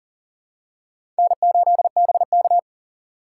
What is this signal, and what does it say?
Morse code, heard as a keyed tone, about 40 wpm, 700 Hz. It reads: N8BK
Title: USA Call Signs - 4 Characters - 4x Character Spacing - 40wpm